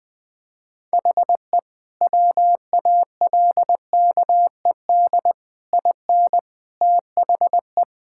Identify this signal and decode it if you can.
Morse code, heard as a keyed tone, about 20 wpm, 700 Hz. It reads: HE WALKED IN THE
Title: Sentences with 5-Word Phrases - 20wpm